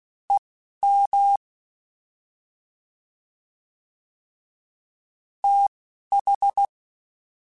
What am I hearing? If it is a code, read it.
Morse code, heard as a keyed tone, about 16 wpm, spaced as for 12 wpm, 800 Hz. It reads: EM TH